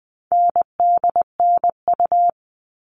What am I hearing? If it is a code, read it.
Morse code, heard as a keyed tone, about 20 wpm, 700 Hz. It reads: NDNU
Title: All Cut Numbers - 4-Digit 20wpm